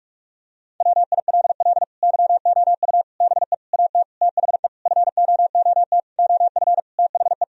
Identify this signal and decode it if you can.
Morse code, heard as a keyed tone, about 46 wpm, 700 Hz. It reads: WILL YOU BE AT THE FOOT OF THE
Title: Sentences from Top 1000 Words - Review Part 3 - 45wpm